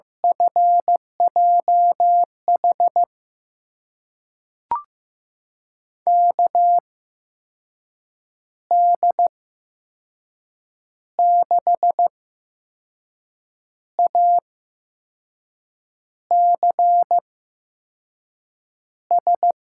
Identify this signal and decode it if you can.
Morse code, heard as a keyed tone, about 15 wpm, spaced as for 5 wpm, 700 Hz. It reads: FJH KD6ACS
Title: USA Call Signs - 6 Characters - 8x Character Spacing - 15wpm